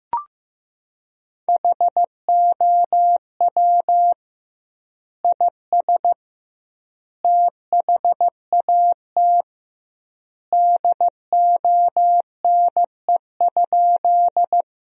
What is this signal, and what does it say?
Morse code, heard as a keyed tone, about 15 wpm, 700 Hz. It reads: HOW IS THAT DONE?
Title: 2x Word-Spacing - Sentences from Top 500 Words - 15wpm